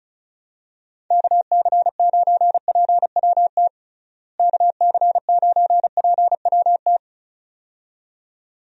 Morse code, heard as a keyed tone, about 36 wpm, 700 Hz. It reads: KC9PWT KC9PWT
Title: USA Call Signs - 6 Characters - Repeated 2x - 35wpm